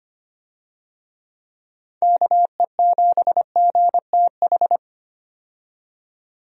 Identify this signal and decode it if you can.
Morse code, heard as a keyed tone, about 25 wpm, 700 Hz. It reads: KE7GTH
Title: USA Call Signs - 6 Characters - 25wpm